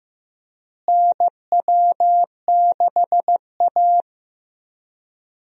Morse code, heard as a keyed tone, about 15 wpm, 700 Hz. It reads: NW6A
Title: USA Call Signs - 4 Characters - Repeated 2x - 15wpm